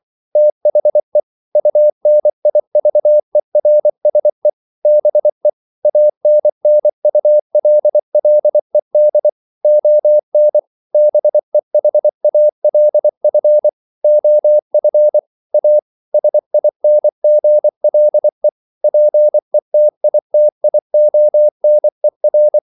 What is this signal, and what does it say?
Morse code, heard as a keyed tone, about 24 wpm, 600 Hz. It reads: THE UNIVERSE BE ANNULLED ON BEHALF OF A SINGLE PETITIONER
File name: CW / Practice / EnglishQuotations0424